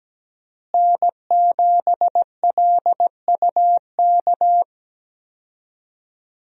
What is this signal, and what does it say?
Morse code, heard as a keyed tone, about 17 wpm, 700 Hz. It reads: N7LUK
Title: USA Call Signs - 5 Characters - Repeated 3x - 17wpm